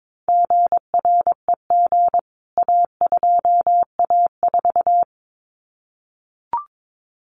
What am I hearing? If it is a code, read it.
Morse code, heard as a keyed tone, about 22 wpm, 700 Hz. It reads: GREG A2A4
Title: CWT Exchange 22wpm